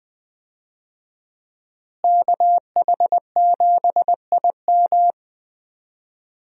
Morse code, heard as a keyed tone, about 20 wpm, 700 Hz. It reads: KH7IM
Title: USA Call Signs - 5 Characters - Repeated 3x - 20wpm